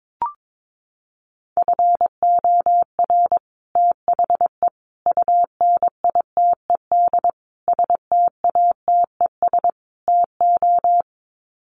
Morse code, heard as a keyed tone, about 22 wpm, 700 Hz. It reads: FOR THE UNITED STATES TO
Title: Sets of 5 Words Encore 22wpm